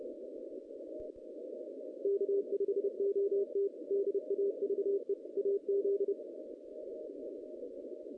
Morse code, heard as a keyed tone, about 29 wpm, 410 Hz. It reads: K5OT DAVE AZ
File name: 2016 JAN CW NAQP / 2016 Jan naqp k5ot - 0-6